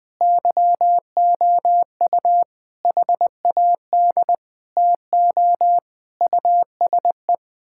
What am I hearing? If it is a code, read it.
Morse code, heard as a keyed tone, about 20 wpm, 700 Hz. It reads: YOU HAD TO USE
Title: Sentences from Top 1000 Words - Review Part 6 - 20wpm